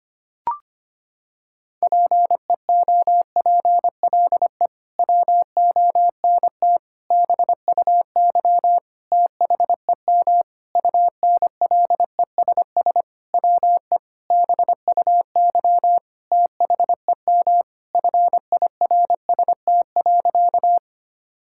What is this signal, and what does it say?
Morse code, heard as a keyed tone, about 25 wpm, 700 Hz. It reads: PEOPLE WONT BUY THEM UNLESS WE BUY THEM FIRST.
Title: Sentences from Top 1000 Words - Review Part 6 - 25wpm